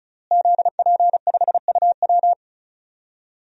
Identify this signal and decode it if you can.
Morse code, heard as a keyed tone, about 36 wpm, 700 Hz. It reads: ZP5UW
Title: DXCC-100 Call Signs - 5 Characters - Repeated 2x - 35wpm